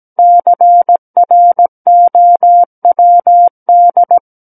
Morse code, heard as a keyed tone, about 17 wpm, 700 Hz. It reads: CROWD
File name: Book One / 08-Chapter 8 - 17 WPM